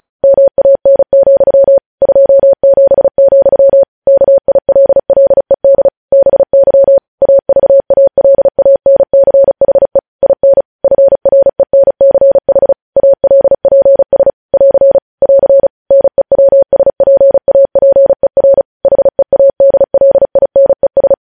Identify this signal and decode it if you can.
Morse code, heard as a keyed tone, about 36 wpm, 550 Hz. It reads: MAN, 27, KILLED BY AVALANCHE IN FRENCH ALPS <AR> <AR> NEWSPAPER HEADLINES
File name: podcast / News headlines-1x-35x35